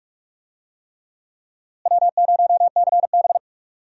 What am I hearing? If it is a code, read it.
Morse code, heard as a keyed tone, about 46 wpm, 700 Hz. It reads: W0CB
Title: USA Call Signs - 4 Characters - Repeated 3x - 45wpm